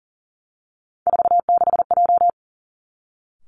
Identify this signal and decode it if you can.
Morse code, heard as a keyed tone, about 39 wpm, 700 Hz. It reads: 46J
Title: Three Challenging Letter-Number Combinations 40wpm